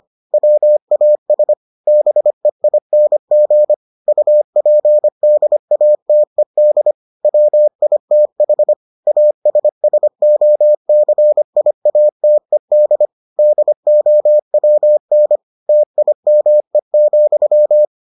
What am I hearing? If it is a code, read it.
Morse code, heard as a keyed tone, about 25 wpm, 600 Hz. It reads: WAS BEING UPDATED WITH ASSOCIATED DOWN TIME,